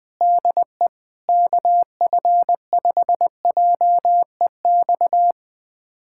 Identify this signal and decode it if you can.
Morse code, heard as a keyed tone, about 21 wpm, 700 Hz. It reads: DE KF5JEX
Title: USA Callsign DE Callsign - 6 Characters - 20wpm